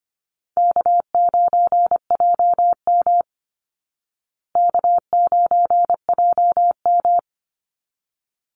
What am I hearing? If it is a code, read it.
Morse code, heard as a keyed tone, about 25 wpm, 700 Hz. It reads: K9JM K9JM
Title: Easy Call Signs Repeated 2x - 25wpm